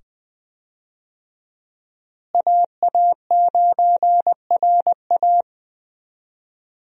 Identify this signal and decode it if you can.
Morse code, heard as a keyed tone, about 20 wpm, 700 Hz. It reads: AA9RA
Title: USA Call Signs - 5 Characters - Repeated 2x - 20wpm